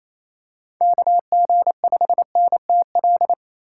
Morse code, heard as a keyed tone, about 28 wpm, 700 Hz. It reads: KG5NTL
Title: USA Call Signs - 8x Character Spacing - 28wpm